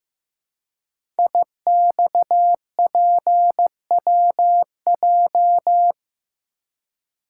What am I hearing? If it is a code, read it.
Morse code, heard as a keyed tone, about 15 wpm, 700 Hz. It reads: IXPWJ
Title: Code Groups - 5 Letters - 15wpm